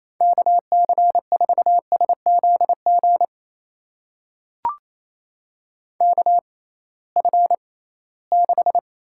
Morse code, heard as a keyed tone, about 28 wpm, spaced as for 18 wpm, 700 Hz. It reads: KC4SZG KF6
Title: USA Call Signs - 6 Characters - 6x Character Spacing - 28wpm